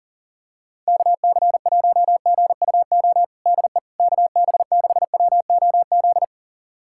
Morse code, heard as a keyed tone, about 40 wpm, 700 Hz. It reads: KC1GUO DE KB6WOZ
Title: USA Callsign DE Callsign 40wpm